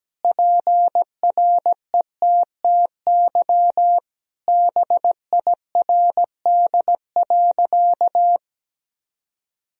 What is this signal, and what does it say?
Morse code, heard as a keyed tone, about 17 wpm, 700 Hz. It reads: PRETTY BIRD.
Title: Sentences from Top 1000 Words - Review Part 2 - 17wpm